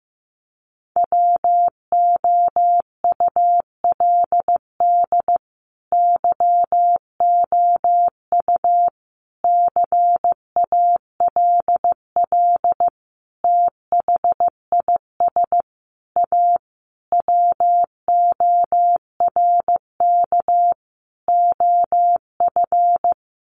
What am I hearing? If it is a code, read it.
Morse code, heard as a keyed tone, about 15 wpm, 700 Hz. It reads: WOULD YOU CALL THIS A WORK OF
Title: Sentences from Top 600 Words 15wpm